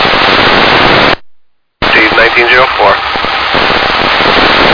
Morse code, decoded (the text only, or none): none